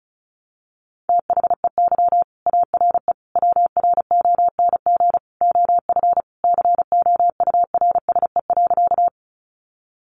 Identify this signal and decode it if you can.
Morse code, heard as a keyed tone, about 34 wpm, 700 Hz. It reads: THEY ARE WRONG OF COURSE.
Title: Sentences from Top 900 Words 35wpm